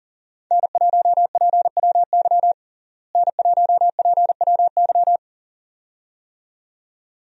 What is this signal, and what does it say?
Morse code, heard as a keyed tone, about 40 wpm, 700 Hz. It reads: N1PWY N1PWY
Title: USA Call Signs - 5 Characters - Repeated 2x - 40wpm